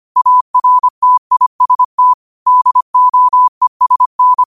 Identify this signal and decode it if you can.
Morse code, heard as a keyed tone, about 25 wpm, 1000 Hz. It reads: ARTIST DOESN
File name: qotdfiles / 25wpm